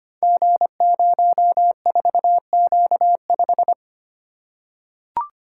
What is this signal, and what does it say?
Morse code, heard as a keyed tone, about 25 wpm, 700 Hz. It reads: G04Q5
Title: Code Groups - 5 Letter:Numbers - 25wpm